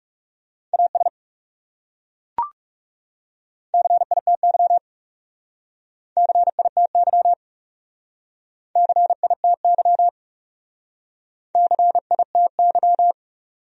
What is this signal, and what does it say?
Morse code, heard as a keyed tone, about 37 wpm, 700 Hz. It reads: AS CITY CITY CITY CITY
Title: Speed-Racing Top 500 Words 30wpm